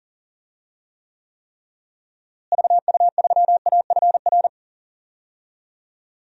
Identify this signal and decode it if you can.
Morse code, heard as a keyed tone, about 40 wpm, 700 Hz. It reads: VU3AFR
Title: DXCC-100 Call Signs - 6 Characters - Repeated 3x - 40wpm